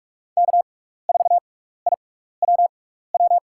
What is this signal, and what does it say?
Morse code, heard as a keyed tone, about 43 wpm, spaced as for 15 wpm, 690 Hz. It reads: K4IWW
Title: USA Call Signs - 5 Characters - 6x Character Spacing - 45wpm